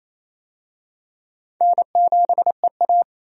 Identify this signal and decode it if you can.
Morse code, heard as a keyed tone, about 28 wpm, 700 Hz. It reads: N7EA
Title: USA Call Signs - 4 Characters - 28wpm